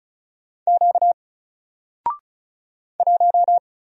Morse code, heard as a keyed tone, about 36 wpm, 700 Hz. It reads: Q 1
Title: Single Character (Letter:Number:Punctuation:Prosign) Alt 35wpm